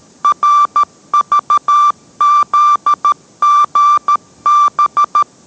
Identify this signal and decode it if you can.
Morse code, heard as a keyed tone, about 13 wpm, 1180 Hz. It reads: RVZGB